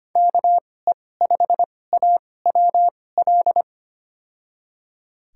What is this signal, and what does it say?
Morse code, heard as a keyed tone, about 25 wpm, spaced as for 18 wpm, 700 Hz. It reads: KE5AWL
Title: USA Call Signs - 6 Characters - 2x Character Spacing - 25wpm